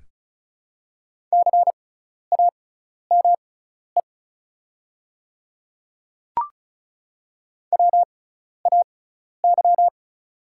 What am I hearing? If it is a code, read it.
Morse code, heard as a keyed tone, about 36 wpm, spaced as for 12 wpm, 700 Hz. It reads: CAME WAY
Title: 6x Character Spacing - Top 200 Words - 35wpm